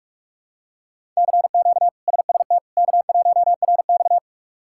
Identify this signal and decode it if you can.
Morse code, heard as a keyed tone, about 46 wpm, 700 Hz. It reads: CQ SST K1RX
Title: K1USN SST QSO Practice - 45wpm